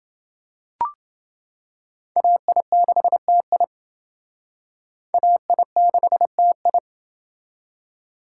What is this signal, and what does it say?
Morse code, heard as a keyed tone, about 28 wpm, 700 Hz. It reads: AI6TI AI6TI